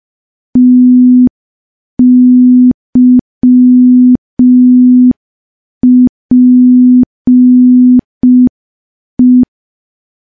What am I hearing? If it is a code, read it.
Morse code, heard as a keyed tone, about 5 wpm, 250 Hz. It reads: TYPE
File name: morse / morse-training-05WPM